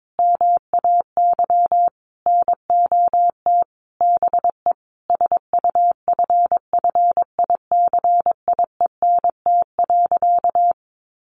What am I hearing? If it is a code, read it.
Morse code, heard as a keyed tone, about 22 wpm, 700 Hz. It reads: MAY NOT BE SUFFICIENT.
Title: Sentences from Top 1400 Words 22wpm